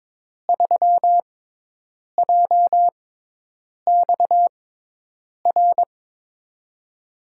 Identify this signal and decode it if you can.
Morse code, heard as a keyed tone, about 22 wpm, spaced as for 8 wpm, 700 Hz. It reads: 3JXR
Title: USA Call Signs - 5 Characters - 6x Character Spacing - 22wpm